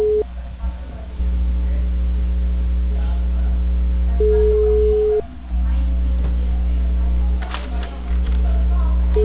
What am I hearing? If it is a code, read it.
Ringback tone (Europe, 425 Hz).
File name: out- -2007- - - .5945